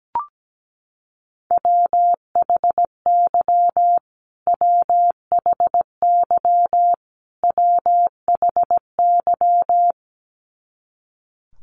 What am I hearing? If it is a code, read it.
Morse code, heard as a keyed tone, about 17 wpm, 700 Hz. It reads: WHY WHY WHY
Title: Top 300 Words Repeated 17wpm